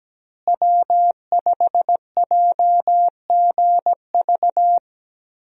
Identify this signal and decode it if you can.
Morse code, heard as a keyed tone, about 17 wpm, 700 Hz. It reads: W5JGV
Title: USA Call Signs - 5 Characters - Repeated 2x - 17wpm